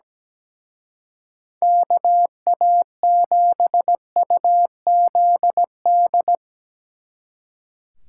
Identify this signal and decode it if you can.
Morse code, heard as a keyed tone, about 18 wpm, 700 Hz. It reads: KA7UZD
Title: Speed Racing - USA 2x3 Call Signs 17wpm